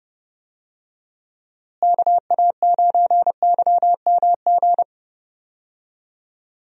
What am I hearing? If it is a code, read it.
Morse code, heard as a keyed tone, about 30 wpm, 700 Hz. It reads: KA9YMG
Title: USA Call Signs - 6 Characters - Repeated 3x - 30wpm